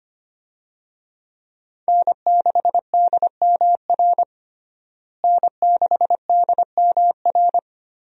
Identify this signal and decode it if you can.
Morse code, heard as a keyed tone, about 25 wpm, 700 Hz. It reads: N6DMR N6DMR
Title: USA Call Signs - 5 Characters - Repeated 3x - 25wpm